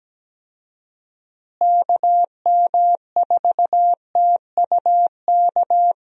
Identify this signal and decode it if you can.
Morse code, heard as a keyed tone, about 17 wpm, 700 Hz. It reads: KM4TUK
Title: USA Call Signs - 6 Characters - Repeated 3x - 17wpm